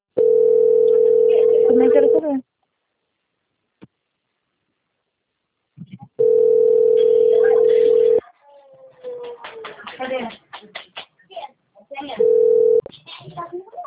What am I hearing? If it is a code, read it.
Ringback tone (North America), 440 + 480 Hz, with speech.
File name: out- - - - - .27658